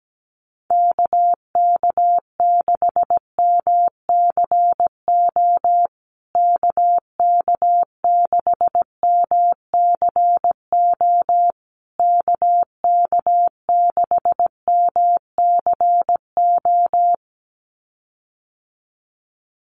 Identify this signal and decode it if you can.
Morse code, heard as a keyed tone, about 17 wpm, 700 Hz. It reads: KK6MCO KK6MCO KK6MCO
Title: USA Call Signs Repeated 3x 17wpm